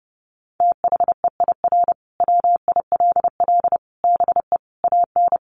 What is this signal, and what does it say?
Morse code, heard as a keyed tone, about 30 wpm, 700 Hz. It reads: THEIR WILL BE AN
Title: Sentences from Top 1500 Words 30wpm